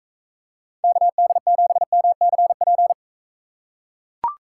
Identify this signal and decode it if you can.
Morse code, heard as a keyed tone, about 43 wpm, 700 Hz. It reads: KD7MCP